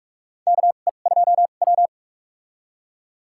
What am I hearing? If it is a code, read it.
Morse code, heard as a keyed tone, about 43 wpm, spaced as for 32 wpm, 700 Hz. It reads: KE2W
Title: USA Call Signs - 4 Characters - 2x Character Spacing - 45wpm